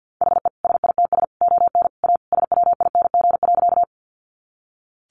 Morse code, heard as a keyed tone, about 50 wpm, 700 Hz. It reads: HE SITS ON A SWING.
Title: Additional Easy Phrases 50wpm